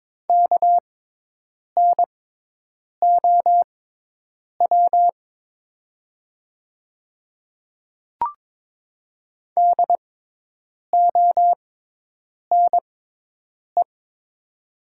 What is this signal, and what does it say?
Morse code, heard as a keyed tone, about 22 wpm, spaced as for 8 wpm, 700 Hz. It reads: KNOW DONE
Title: 6x Character Spacing - Top 500 Words - 22wpm